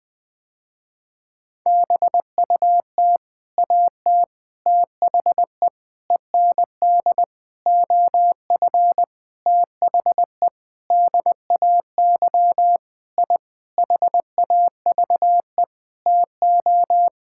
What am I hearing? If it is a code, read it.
Morse code, heard as a keyed tone, about 19 wpm, 700 Hz. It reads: BUT AT THE END OF THE DAY I HAVE TO
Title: Sentences with 8-Word Phrases - Spoken Beginning - 20wpm